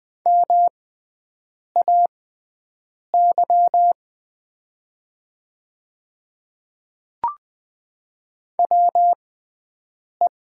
Morse code, heard as a keyed tone, about 20 wpm, spaced as for 7 wpm, 700 Hz. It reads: MAY WE
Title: 6x Character Spacing - Top 100 Words - 20wpm